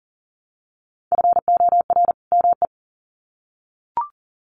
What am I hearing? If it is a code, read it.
Morse code, heard as a keyed tone, about 39 wpm, 700 Hz. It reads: FOR ME
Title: Sets of 2 Words 40wpm